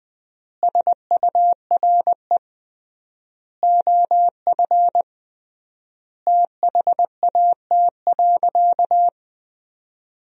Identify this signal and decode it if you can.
Morse code, heard as a keyed tone, about 20 wpm, 700 Hz. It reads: SURE OF THAT.
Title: 3x Word-Spacing - Sentences from Top 300 Words - 20wpm